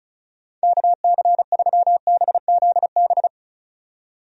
Morse code, heard as a keyed tone, about 36 wpm, 700 Hz. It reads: KC3BZB
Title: USA Call Signs - 6 Characters - 35wpm